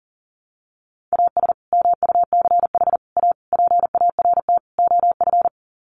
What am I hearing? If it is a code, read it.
Morse code, heard as a keyed tone, about 40 wpm, 700 Hz. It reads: AS MUCH A PART OF
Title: Sets of 5 Words Encore2 40wpm